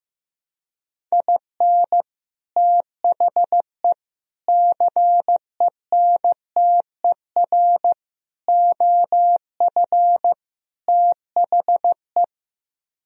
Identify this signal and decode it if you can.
Morse code, heard as a keyed tone, about 15 wpm, 700 Hz. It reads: IN THE CENTER OF THE
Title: Speed-Racing - Sets of 5 Words 15wpm